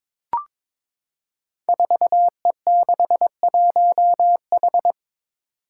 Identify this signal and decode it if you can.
Morse code, heard as a keyed tone, about 22 wpm, 700 Hz. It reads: 4E61H
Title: Five Challenging Letter-Number Combinations 22wpm